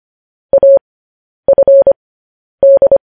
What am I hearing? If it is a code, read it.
Morse code, heard as a keyed tone, about 25 wpm, spaced as for 10 wpm, 550 Hz. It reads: AFD
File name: podcast / News headlines-1x-25x10